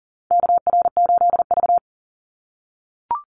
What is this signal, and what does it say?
Morse code, heard as a keyed tone, about 38 wpm, 700 Hz. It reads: KR8V
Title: Call Signs - 40wpm